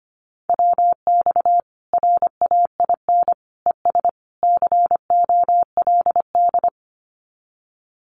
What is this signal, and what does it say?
Morse code, heard as a keyed tone, about 25 wpm, 700 Hz. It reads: WX RAIN ES COLD
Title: Common QSO Phrases 25wpm